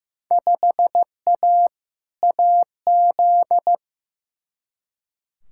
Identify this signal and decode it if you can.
Morse code, heard as a keyed tone, about 15 wpm, 700 Hz. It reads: 5A AZ
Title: Field Day Exchanges - 15wpm